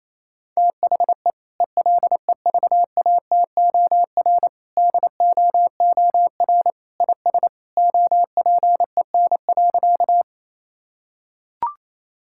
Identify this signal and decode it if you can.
Morse code, heard as a keyed tone, about 28 wpm, 700 Hz. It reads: THE ELEVATOR DOOR IS OPEN.
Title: Sentences from Top 1600 Words 28wpm